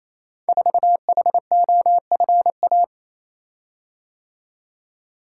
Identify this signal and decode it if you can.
Morse code, heard as a keyed tone, about 29 wpm, 700 Hz. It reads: 4HOFA
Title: Code Groups - 5 Letter:Numbers - 28wpm